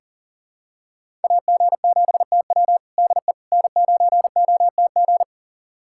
Morse code, heard as a keyed tone, about 40 wpm, 700 Hz. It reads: AG7TW DE N9OTG
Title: USA Callsign DE Callsign - 5 Characters - 40wpm